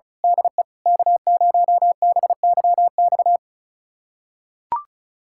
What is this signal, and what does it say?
Morse code, heard as a keyed tone, about 36 wpm, 700 Hz. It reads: DE K0BYX
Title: USA Callsign DE Callsign - 5 Characters - 35wpm